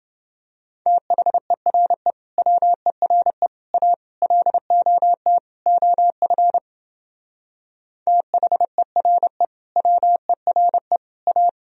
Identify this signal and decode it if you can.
Morse code, heard as a keyed tone, about 29 wpm, 700 Hz. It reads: THERE WERE A LOT OF THERE WERE A